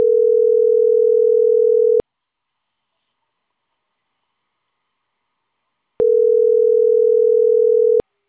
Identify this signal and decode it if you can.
Ringback tone (North America), 440 + 480 Hz.